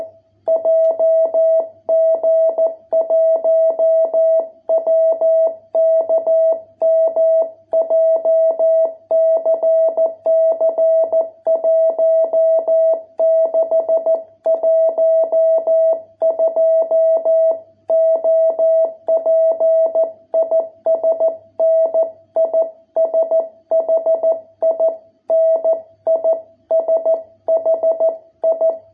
Morse code, heard as a keyed tone, about 14 wpm, 660 Hz. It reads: JG1WKM JCC 1612 OP IS NISHI NISHI